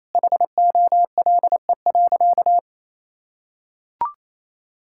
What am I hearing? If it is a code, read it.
Morse code, heard as a keyed tone, about 28 wpm, 700 Hz. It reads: HOLE.
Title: Sentences from Top 1600 Words 28wpm